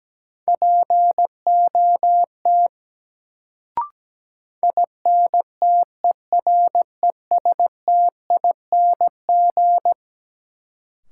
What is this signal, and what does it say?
Morse code, heard as a keyed tone, about 17 wpm, 700 Hz. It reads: POT INTERESTING
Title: Sentences from Top 1000 Words - New Word Drill - 17wpm